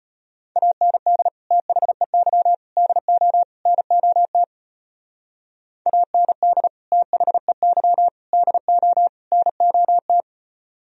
Morse code, heard as a keyed tone, about 32 wpm, 700 Hz. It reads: AND THEY DO NOT AND THEY DO NOT